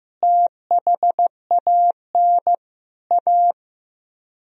Morse code, heard as a keyed tone, about 15 wpm, 700 Hz. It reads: THAN A
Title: Speed-Racing - Sets of 4 Words 15wpm